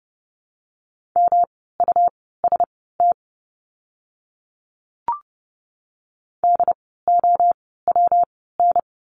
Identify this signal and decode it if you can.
Morse code, heard as a keyed tone, about 30 wpm, spaced as for 17 wpm, 700 Hz. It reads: MUST DOWN
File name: course / 30:10wpm / 115 Lesson 19f - Letter w - Words 30wpm